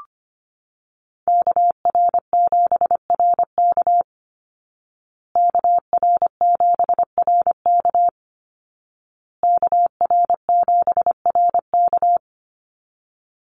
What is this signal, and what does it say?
Morse code, heard as a keyed tone, about 25 wpm, 700 Hz. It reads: KR7RK KR7RK KR7RK
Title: Call Signs Repeated 2x - 25wpm